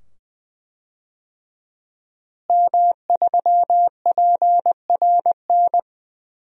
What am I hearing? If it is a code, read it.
Morse code, heard as a keyed tone, about 20 wpm, 700 Hz. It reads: M3PRN
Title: DXCC-100 Call Signs - 5 Characters - Repeated 2x - 20wpm